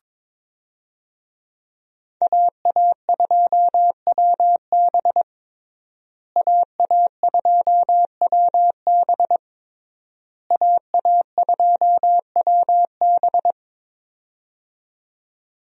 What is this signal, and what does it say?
Morse code, heard as a keyed tone, about 21 wpm, 700 Hz. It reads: AA2WB AA2WB AA2WB
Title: USA Call Signs - 5 Characters - Repeated 3x - 22wpm